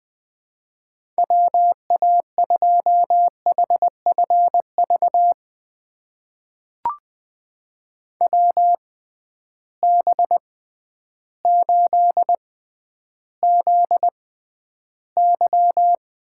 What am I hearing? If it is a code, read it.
Morse code, heard as a keyed tone, about 20 wpm, spaced as for 11 wpm, 700 Hz. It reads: WA2HFV WB8ZY
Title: USA Call Signs - 6 Characters - 6x Character Spacing - 20wpm